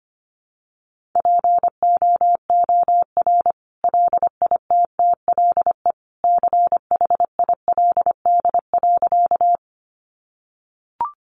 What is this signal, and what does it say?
Morse code, heard as a keyed tone, about 25 wpm, 700 Hz. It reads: POOR LITTLE CHILD.
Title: Sentences from Top 1000 Words 25wpm